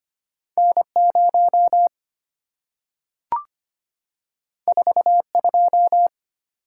Morse code, heard as a keyed tone, about 25 wpm, 690 Hz. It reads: N0 42
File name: Two Letter-Number Combinations - 25wpm